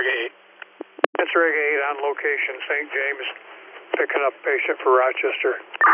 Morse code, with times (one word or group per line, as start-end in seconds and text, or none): none